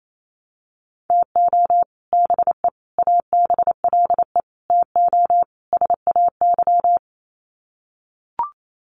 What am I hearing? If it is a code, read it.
Morse code, heard as a keyed tone, about 28 wpm, 700 Hz. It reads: TO BE ABLE TO SAY
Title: Sets of 5 Words Encore2 28wpm